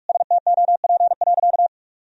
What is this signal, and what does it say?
Morse code, heard as a keyed tone, about 41 wpm, 700 Hz. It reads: STOP.
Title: Sentences from Top 1000 Words - Review Part 1 - 45wpm